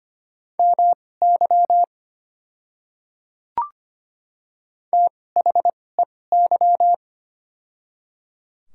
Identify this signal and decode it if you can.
Morse code, heard as a keyed tone, about 25 wpm, spaced as for 18 wpm, 700 Hz. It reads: MY THEY
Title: 2x Character Spacing - Top 100 Words - 25wpm